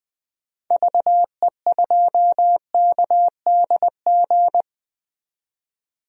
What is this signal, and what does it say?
Morse code, heard as a keyed tone, about 20 wpm, 700 Hz. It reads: VE2KDG
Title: DXCC-100 Call Signs - 6 Characters - 20wpm